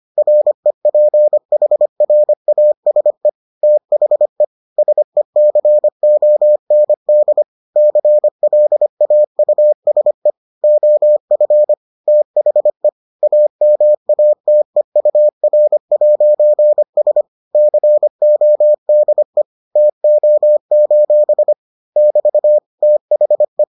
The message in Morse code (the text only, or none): REPHRASE THE SECOND CLAUSE OF THE AMATEUR'S CODE TO: <BT> THE